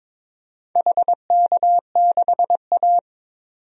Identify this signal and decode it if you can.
Morse code, heard as a keyed tone, about 22 wpm, 700 Hz. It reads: HK6A
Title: DXCC-100 Call Signs - 4 Characters - 22wpm